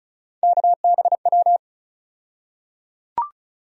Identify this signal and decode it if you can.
Morse code, heard as a keyed tone, about 36 wpm, 700 Hz. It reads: KBW
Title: USA Call Sign Postfixes - 3 Letters 35wpm